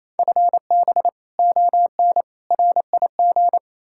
Morse code, heard as a keyed tone, about 37 wpm, 700 Hz. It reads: FB ON RIG